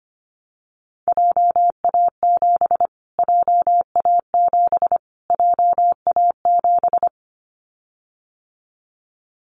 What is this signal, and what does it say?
Morse code, heard as a keyed tone, about 25 wpm, 700 Hz. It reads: JA7 JA7 JA7
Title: Callsign Prefixes for Top 25 DXCC - Repeated 3x - 25wpm